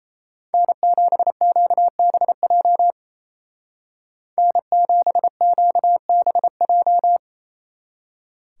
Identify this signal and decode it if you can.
Morse code, heard as a keyed tone, about 34 wpm, 700 Hz. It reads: N7QBJ N7QBJ
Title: Speed Racing - USA Call Signs 28wpm